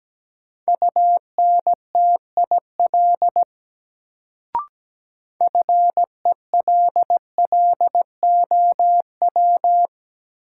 Morse code, heard as a keyed tone, about 17 wpm, 700 Hz. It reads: UNTIL FELLOW
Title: Sets of 5 Words Encore - New Word Drill - 17wpm